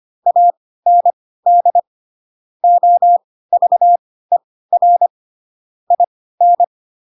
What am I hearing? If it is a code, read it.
Morse code, heard as a keyed tone, about 25 wpm, spaced as for 16 wpm, 700 Hz. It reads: AND OVER IN